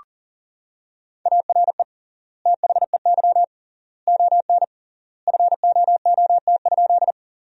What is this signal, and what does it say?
Morse code, heard as a keyed tone, about 37 wpm, 700 Hz. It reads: ARE THEY ON FOOT?
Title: 3x Word-Spacing - Sentences from Top 500 Words - 40wpm